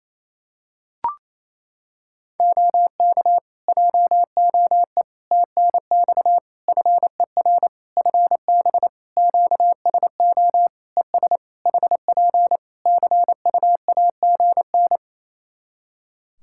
Morse code, heard as a keyed tone, about 28 wpm, 700 Hz. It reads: OK JOE TNX FER FB QSO ES HP CUAGN
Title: Practice QSO - Exchange 4 - Ending - 28wpm